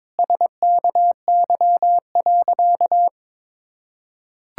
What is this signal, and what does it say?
Morse code, heard as a keyed tone, about 21 wpm, 700 Hz. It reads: SKY.
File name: Sentences with 9-Word Phrases - 22wpm